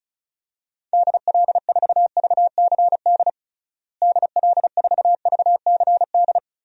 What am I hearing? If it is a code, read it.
Morse code, heard as a keyed tone, about 36 wpm, 700 Hz. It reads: DL4VCD DL4VCD
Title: DXCC-100 Call Signs - 6 Characters - Repeated 2x - 35wpm